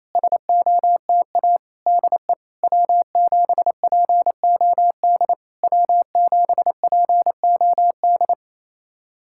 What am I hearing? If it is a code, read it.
Morse code, heard as a keyed tone, about 28 wpm, 700 Hz. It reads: SOTA DE W7POD W7POD
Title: SOTA QSO Practice 28wpm